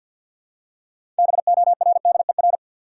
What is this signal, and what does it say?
Morse code, heard as a keyed tone, about 53 wpm, 700 Hz. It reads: BORDER
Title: Six Letter Words - 50wpm